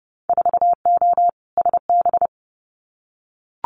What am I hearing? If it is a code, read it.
Morse code, heard as a keyed tone, about 30 wpm, 700 Hz. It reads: 4O SB
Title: Winter Field Day Exchanges 30wpm